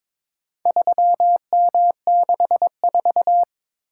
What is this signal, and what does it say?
Morse code, heard as a keyed tone, about 22 wpm, 700 Hz. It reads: 3M64
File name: practice / 22wpm / Four Letter-Number Combinations - 22wpm